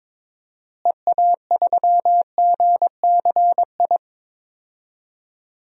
Morse code, heard as a keyed tone, about 21 wpm, 700 Hz. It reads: EA3GCI
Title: DXCC-100 Call Signs - 6 Characters - Repeated 2x - 22wpm